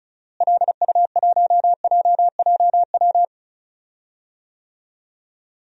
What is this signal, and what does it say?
Morse code, heard as a keyed tone, about 34 wpm, 700 Hz. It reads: LU1JJW
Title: DXCC-100 Call Signs - 6 Characters - Repeated 3x - 35wpm